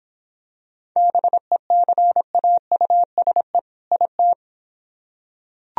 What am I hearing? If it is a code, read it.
Morse code, heard as a keyed tone, about 26 wpm, 700 Hz. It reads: BECAUSE IT